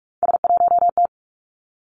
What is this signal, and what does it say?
Morse code, heard as a keyed tone, about 46 wpm, 700 Hz. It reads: S1T
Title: Three Challenging Letter-Number Combinations 45wpm